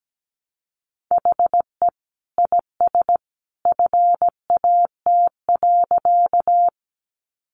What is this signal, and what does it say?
Morse code, heard as a keyed tone, about 17 wpm, 700 Hz. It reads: HE IS FAT.
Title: Easy Phrases 17wpm